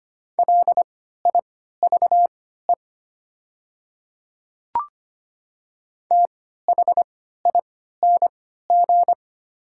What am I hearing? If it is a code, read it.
Morse code, heard as a keyed tone, about 25 wpm, spaced as for 14 wpm, 700 Hz. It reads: LIVE THING
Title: 3x Character Spacing - Top 200 Words - 25wpm